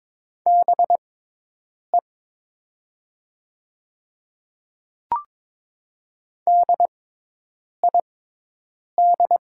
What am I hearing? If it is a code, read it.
Morse code, heard as a keyed tone, about 22 wpm, spaced as for 8 wpm, 700 Hz. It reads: BE DID
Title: 6x Character Spacing - Top 100 Words - 22wpm